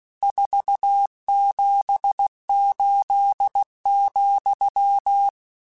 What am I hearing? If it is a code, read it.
Morse code, heard as a keyed tone, about 16 wpm, 800 Hz. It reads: 478,
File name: RSTs / RST-mp3s / Your RSTs @15